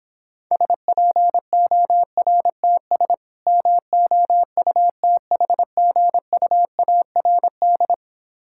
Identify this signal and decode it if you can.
Morse code, heard as a keyed tone, about 26 wpm, 700 Hz. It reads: SPORTS MOUTHGUARD